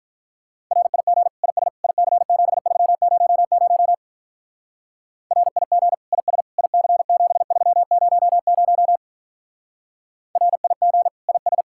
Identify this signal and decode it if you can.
Morse code, heard as a keyed tone, about 46 wpm, 700 Hz. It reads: RIG IS IC7300 RIG IS IC7300 RIG IS